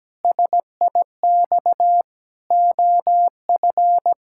Morse code, heard as a keyed tone, about 17 wpm, 700 Hz. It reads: SIX OF
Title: Pangram Challenge v2 - 17wpm